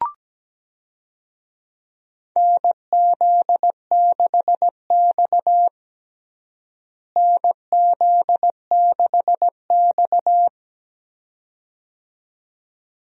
Morse code, heard as a keyed tone, about 17 wpm, 700 Hz. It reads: NZ6X NZ6X
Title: USA Call Signs - 4 Characters - Repeated 2x - 17wpm